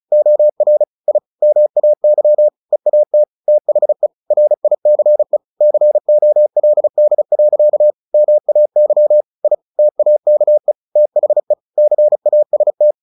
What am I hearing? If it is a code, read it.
Morse code, heard as a keyed tone, about 34 wpm, 600 Hz. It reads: OR I MAY EAT THE RICE COLD. MAY I TAKE THE CAST
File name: the old man and the sea / the old man and the sea 1 35wpm